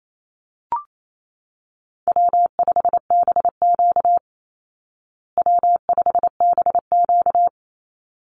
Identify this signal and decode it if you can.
Morse code, heard as a keyed tone, about 28 wpm, 700 Hz. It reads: W5BQ W5BQ
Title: Easy Call Signs Repeated 2x - 28wpm